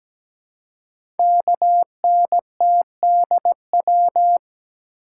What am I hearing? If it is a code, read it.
Morse code, heard as a keyed tone, about 17 wpm, 700 Hz. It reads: KNTDW
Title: Code Groups - 5 Letters - 17wpm